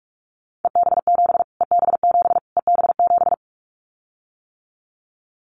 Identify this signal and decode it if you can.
Morse code, heard as a keyed tone, about 44 wpm, 700 Hz. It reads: EB7 EB7 EB7
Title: Callsign Prefixes for Top 25 DXCC - Repeated 3x - 45wpm